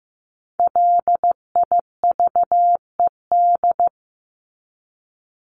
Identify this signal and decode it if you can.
Morse code, heard as a keyed tone, about 15 wpm, 700 Hz. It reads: LIVED
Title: ED Suffix - Top 100 Words with Base Words 15wpm